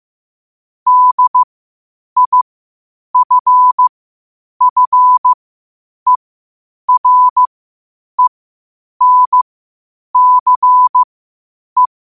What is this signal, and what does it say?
Morse code, heard as a keyed tone, about 15 wpm, spaced as for 8 wpm, 1000 Hz. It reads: DIFFERENCE
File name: 8wpm